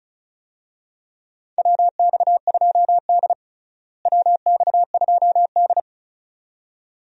Morse code, heard as a keyed tone, about 34 wpm, 700 Hz. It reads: WX2D WX2D
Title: USA Call Signs - 4 Characters - Repeated 2x - 35wpm